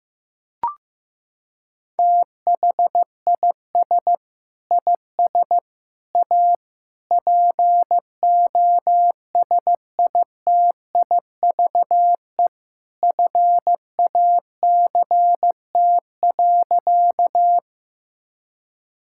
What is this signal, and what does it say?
Morse code, heard as a keyed tone, about 15 wpm, 700 Hz. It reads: THIS IS A POSITIVE FACT.
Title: Sentences from Top 1000 Words - Review Part 5 - 15wpm